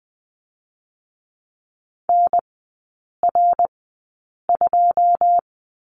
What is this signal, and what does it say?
Morse code, heard as a keyed tone, about 20 wpm, 700 Hz. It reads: N R 2
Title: 114 Lesson 19e - Letter w - Three Characters 20wpm